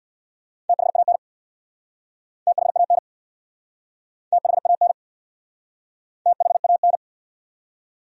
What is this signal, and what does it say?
Morse code, heard as a keyed tone, about 69 wpm, 700 Hz. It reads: THAN THAN THAN THAN